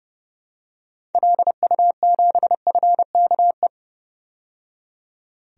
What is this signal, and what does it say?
Morse code, heard as a keyed tone, about 30 wpm, 700 Hz. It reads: LU7FKE
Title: DXCC-100 Call Signs - 6 Characters - 30wpm